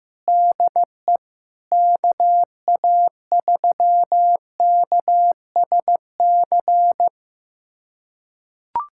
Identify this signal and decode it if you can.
Morse code, heard as a keyed tone, about 15 wpm, 700 Hz. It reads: DE KA3KSC
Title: USA Callsign DE Callsign 15wpm